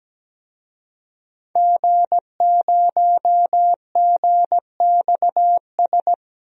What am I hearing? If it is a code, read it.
Morse code, heard as a keyed tone, about 17 wpm, 700 Hz. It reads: G0GXS
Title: DXCC-100 Call Signs - 5 Characters - 17wpm